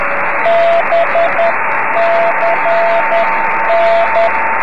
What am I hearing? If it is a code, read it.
Morse code, heard as a keyed tone, about 10 wpm, 680 Hz, with radio noise. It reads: BCN
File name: om3pv 70mhz / zaznam / z35msk